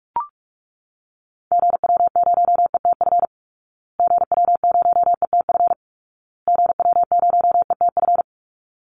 Morse code, heard as a keyed tone, about 44 wpm, 700 Hz. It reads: GW0ETF GW0ETF GW0ETF
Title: Call Signs Repeated 2x - 45wpm